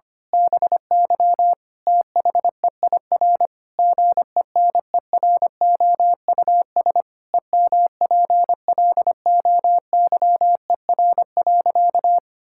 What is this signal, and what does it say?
Morse code, heard as a keyed tone, about 25 wpm, 700 Hz. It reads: BY THEIR GENEROUS EMPLOYER.
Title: Pangram Challenge v2 - 25wpm